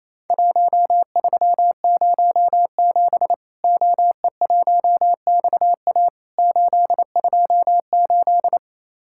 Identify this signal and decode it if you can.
Morse code, heard as a keyed tone, about 28 wpm, 700 Hz. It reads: 1307 OE1XA 828
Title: Worked All Europe DX Contest - QTC Exchange - 28wpm